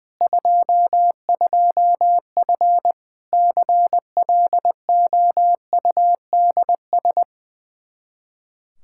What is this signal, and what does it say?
Morse code, heard as a keyed tone, about 20 wpm, 700 Hz. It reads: 22F CLOUDS
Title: Practice QSO - Exchange 2 - Info - 20wpm